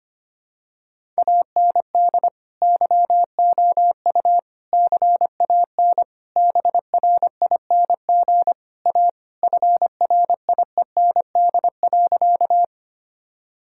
Morse code, heard as a keyed tone, about 25 wpm, 700 Hz. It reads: AND YOU CAN BRING A FRIEND.
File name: practice / 25wpm / Sentences with 2-Word Phrases - 25wpm